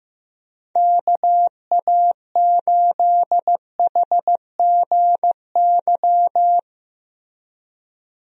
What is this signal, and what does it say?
Morse code, heard as a keyed tone, about 15 wpm, 700 Hz. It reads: KA8HGY
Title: USA Call Signs - 6 Characters - Repeated 2x - 15wpm